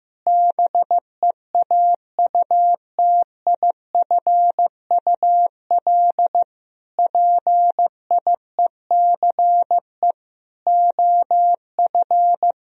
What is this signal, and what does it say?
Morse code, heard as a keyed tone, about 16 wpm, 700 Hz. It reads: BEAUTIFUL PIECE OF
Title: Sentences from Top 1000 Words - Review Part 2 - 15wpm